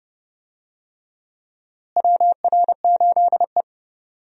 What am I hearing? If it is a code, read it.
Morse code, heard as a keyed tone, about 30 wpm, 700 Hz. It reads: WR8E
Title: USA Call Signs - 4 Characters - Repeated 3x - 30wpm